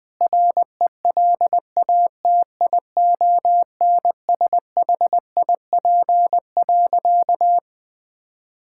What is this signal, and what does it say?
Morse code, heard as a keyed tone, about 21 wpm, 700 Hz. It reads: RELATIONSHIP.
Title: Sentences with 6-Word Phrases - 20wpm